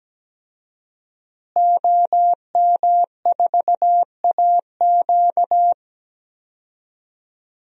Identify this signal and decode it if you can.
Morse code, heard as a keyed tone, about 17 wpm, 700 Hz. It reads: OM4AQ
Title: DXCC-100 Call Signs - 5 Characters - Repeated 3x - 17wpm